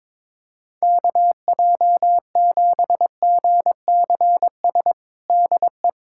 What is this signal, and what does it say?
Morse code, heard as a keyed tone, about 22 wpm, 700 Hz. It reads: KJ7GCS DE
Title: USA Callsign DE Callsign 22wpm